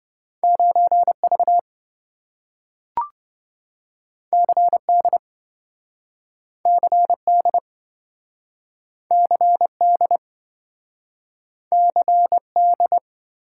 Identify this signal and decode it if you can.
Morse code, heard as a keyed tone, about 27 wpm, 700 Hz. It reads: 9V CD CD CD CD